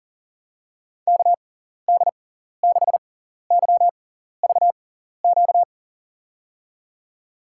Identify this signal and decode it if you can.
Morse code, heard as a keyed tone, about 37 wpm, spaced as for 14 wpm, 700 Hz. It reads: KD6YVQ
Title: USA Call Signs - 6 Characters - 6x Character Spacing - 40wpm